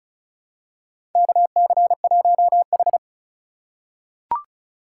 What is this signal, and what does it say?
Morse code, heard as a keyed tone, about 35 wpm, 700 Hz. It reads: KC1H
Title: USA Call Signs - 4 Characters - 2x Character Spacing - 35wpm